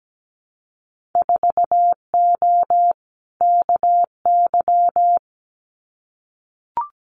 Morse code, heard as a keyed tone, about 17 wpm, 700 Hz. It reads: 4O KY
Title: Winter Field Day Exchanges Encore 17wpm